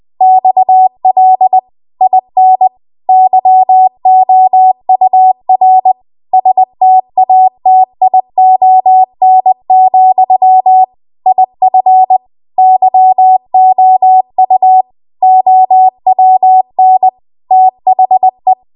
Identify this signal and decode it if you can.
Morse code, heard as a keyed tone, about 20 wpm, 750 Hz. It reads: XL IN YOUR STATION, IF YOU OWN THE
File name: Archive / 20 WPM / 20WPM